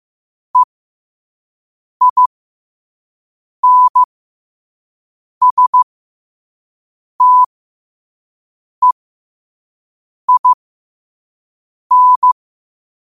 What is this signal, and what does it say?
Morse code, heard as a keyed tone, about 15 wpm, spaced as for 5 wpm, 1000 Hz. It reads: EINSTEIN
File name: qotdfiles / 5wpm